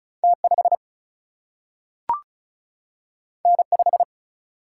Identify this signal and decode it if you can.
Morse code, heard as a keyed tone, about 36 wpm, 700 Hz. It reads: T 5 N 5
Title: Single-Letter Single-Number 35wpm